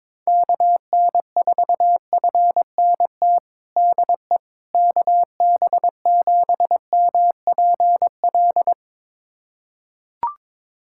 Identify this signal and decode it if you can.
Morse code, heard as a keyed tone, about 22 wpm, 700 Hz. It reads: KN4FNT DE KB7MPL
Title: USA Callsign DE Callsign - 6 Characters - 22wpm